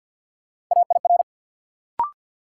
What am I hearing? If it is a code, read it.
Morse code, heard as a keyed tone, about 53 wpm, 700 Hz. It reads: AIR
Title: Sets of 2 Words - New Word Drill - 50wpm